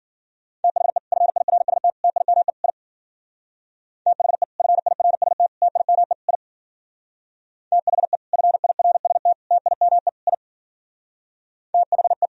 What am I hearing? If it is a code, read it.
Morse code, heard as a keyed tone, about 51 wpm, 700 Hz. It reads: THE FIRST TIME I THE FIRST TIME I THE FIRST TIME I THE